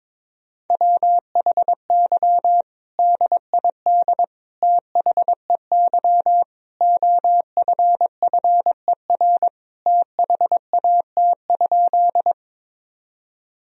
Morse code, heard as a keyed tone, about 22 wpm, 700 Hz. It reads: WHY DID THEY OFFER THAT?
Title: Sentences from Top 1000 Words - Review Part 4 - 22wpm